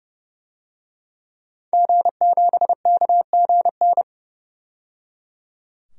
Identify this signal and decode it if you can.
Morse code, heard as a keyed tone, about 30 wpm, 700 Hz. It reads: G7KGN
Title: DXCC-100 Call Signs - 5 Characters - 30wpm